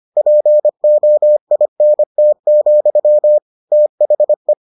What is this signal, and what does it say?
Morse code, heard as a keyed tone, about 25 wpm, 600 Hz. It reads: POINT, THE